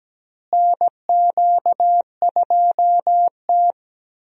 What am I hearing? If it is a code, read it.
Morse code, heard as a keyed tone, about 17 wpm, 700 Hz. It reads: NQ2T
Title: USA Call Signs - 4 Characters - Repeated 3x - 17wpm